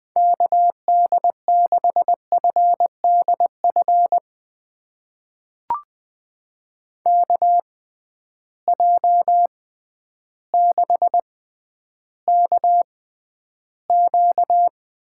Morse code, heard as a keyed tone, about 20 wpm, spaced as for 11 wpm, 700 Hz. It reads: KD6FDF KJ6KQ
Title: USA Call Signs - 6 Characters - 6x Character Spacing - 20wpm